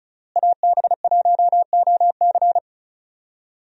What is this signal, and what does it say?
Morse code, heard as a keyed tone, about 36 wpm, 700 Hz. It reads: AB1OC
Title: USA Call Signs - 5 Characters - Repeated 3x - 35wpm